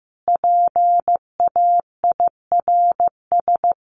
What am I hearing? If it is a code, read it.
Morse code, heard as a keyed tone, about 15 wpm, 700 Hz. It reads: PAIRS
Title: Additional Easy Phrases 15wpm